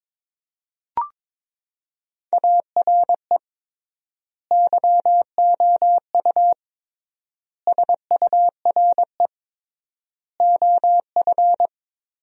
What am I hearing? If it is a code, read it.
Morse code, heard as a keyed tone, about 22 wpm, 700 Hz. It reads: ARE YOU SURE OF
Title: 3x Word-Spacing - Sentences from Top 300 Words - 22wpm